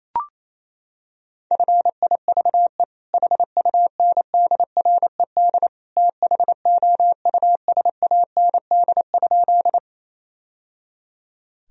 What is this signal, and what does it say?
Morse code, heard as a keyed tone, about 28 wpm, 700 Hz. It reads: FIVE HUNDRED THOUSAND?
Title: Sentences from Top 1000 Words - Review Part 3 - 28wpm